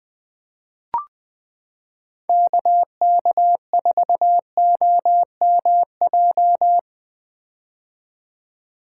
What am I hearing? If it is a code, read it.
Morse code, heard as a keyed tone, about 21 wpm, 700 Hz. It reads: KK4OMJ
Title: 2023 POTA Hunter Call Signs - 20wpm